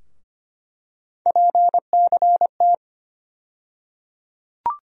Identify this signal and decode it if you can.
Morse code, heard as a keyed tone, about 25 wpm, 700 Hz. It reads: PCT
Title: USA Call Sign Postfixes - 3 Letters 25wpm